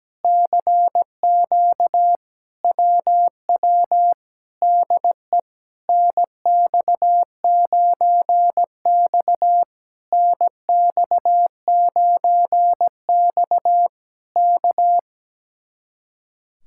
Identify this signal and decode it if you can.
Morse code, heard as a keyed tone, about 17 wpm, 700 Hz. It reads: CQ WW DE NX9X NX9X K
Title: Speed-Racing Common QSO Phrases - 17wpm